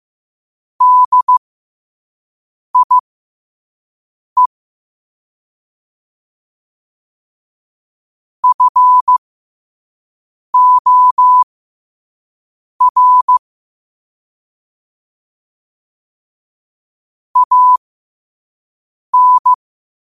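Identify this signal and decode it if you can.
Morse code, heard as a keyed tone, about 15 wpm, spaced as for 5 wpm, 1000 Hz. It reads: DIE FOR AN
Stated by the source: Morse code at 5 words per minute.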